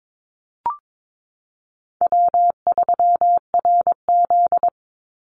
Morse code, heard as a keyed tone, about 22 wpm, 690 Hz. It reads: W3RZ
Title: Easy Call Signs Repeated 2x - 22wpm